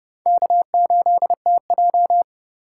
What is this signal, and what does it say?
Morse code, heard as a keyed tone, about 29 wpm, 700 Hz. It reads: K8TJ